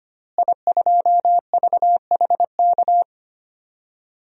Morse code, heard as a keyed tone, about 25 wpm, 700 Hz. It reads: I2VHK
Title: DXCC-100 Call Signs - 5 Characters - Repeated 3x - 25wpm